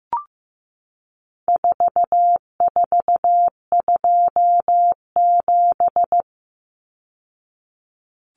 Morse code, heard as a keyed tone, about 15 wpm, 700 Hz. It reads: 4427
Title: Numbers - 4-Digit 15wpm